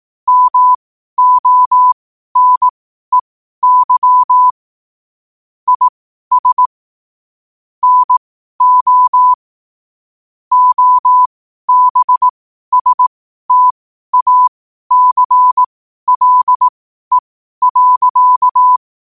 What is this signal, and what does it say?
Morse code, heard as a keyed tone, about 18 wpm, spaced as for 12 wpm, 1000 Hz. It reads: MONEY IS NO OBSTACLE.
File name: qotdfiles / 12wpm